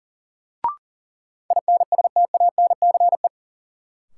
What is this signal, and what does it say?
Morse code, heard as a keyed tone, about 40 wpm, 700 Hz. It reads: INSTANCE
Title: Sentences from Top 800 Words - New Word Drill - 40wpm